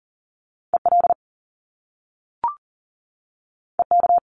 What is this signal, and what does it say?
Morse code, heard as a keyed tone, about 38 wpm, 700 Hz. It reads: EL EK
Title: Top Two Letter Combinations in Words 40wpm